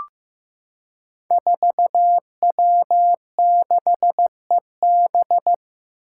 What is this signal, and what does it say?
Morse code, heard as a keyed tone, about 15 wpm, 700 Hz. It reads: 4W6EB
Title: Five Challenging Letter-Number Combinations 15wpm